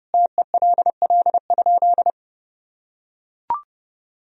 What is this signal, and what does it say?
Morse code, heard as a keyed tone, about 32 wpm, 700 Hz. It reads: TELL?
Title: Sentences from Top 300 Words and Abbreviations 30wpm